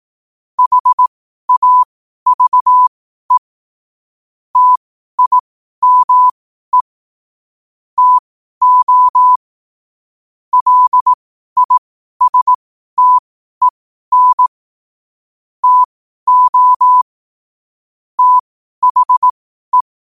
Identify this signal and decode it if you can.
Morse code, heard as a keyed tone, about 18 wpm, spaced as for 12 wpm, 1000 Hz. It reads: HAVE TIME TO LISTEN TO THE
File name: qotdfiles / 12wpm